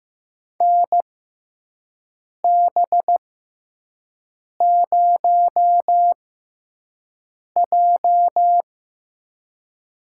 Morse code, heard as a keyed tone, about 15 wpm, spaced as for 5 wpm, 700 Hz. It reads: NB0J
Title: USA Call Signs - 4 Characters - 6x Character Spacing - 15wpm